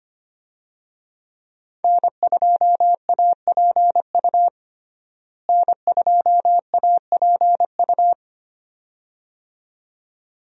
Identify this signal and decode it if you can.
Morse code, heard as a keyed tone, about 25 wpm, 700 Hz. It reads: N2APU N2APU
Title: USA Call Signs - 5 Characters - Repeated 2x - 25wpm